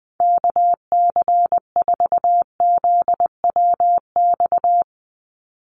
Morse code, heard as a keyed tone, about 20 wpm, 700 Hz. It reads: KC4ZWX
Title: USA Call Signs 20wpm